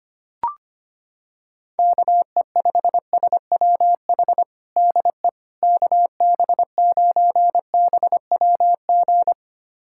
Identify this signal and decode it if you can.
Morse code, heard as a keyed tone, about 25 wpm, 700 Hz. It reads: KE5SWH DE KB9BWG
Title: USA Callsign DE Callsign - 6 Characters - 25wpm